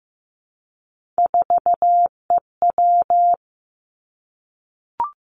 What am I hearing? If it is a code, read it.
Morse code, heard as a keyed tone, about 15 wpm, 700 Hz. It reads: 4EW
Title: Three Challenging Letter-Number Combinations 15wpm